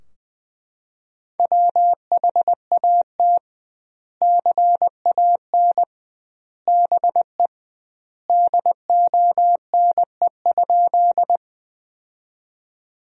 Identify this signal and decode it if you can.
Morse code, heard as a keyed tone, about 20 wpm, 700 Hz. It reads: WHAT CAN BE DONE?
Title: 2x Word-Spacing - Sentences from Top 500 Words - 20wpm